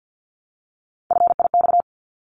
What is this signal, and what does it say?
Morse code, heard as a keyed tone, about 50 wpm, 700 Hz. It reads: FIX
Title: Top 1251-1500 Words 50wpm